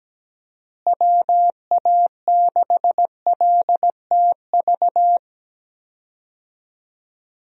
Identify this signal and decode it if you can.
Morse code, heard as a keyed tone, about 17 wpm, 700 Hz. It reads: WA6LTV
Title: USA Call Signs - 6 Characters - Repeated 3x - 17wpm